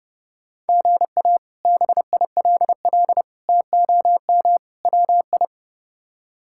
Morse code, heard as a keyed tone, about 30 wpm, 700 Hz. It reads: GA BILL TOM WI
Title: K1USN SST QSO Practice - 30wpm